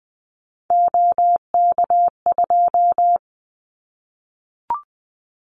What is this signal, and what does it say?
Morse code, heard as a keyed tone, about 20 wpm, 700 Hz. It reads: OK2
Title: Callsign Prefixes for Top 100 DXCC - 20wpm